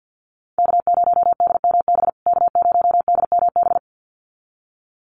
Morse code, heard as a keyed tone, about 50 wpm, 700 Hz. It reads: K0DMB K0DMB
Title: USA Call Signs Repeated 2x 50wpm